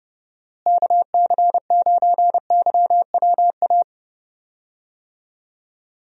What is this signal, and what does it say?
Morse code, heard as a keyed tone, about 31 wpm, 700 Hz. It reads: KC9YWA
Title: USA Call Signs - 6 Characters - Repeated 2x - 30wpm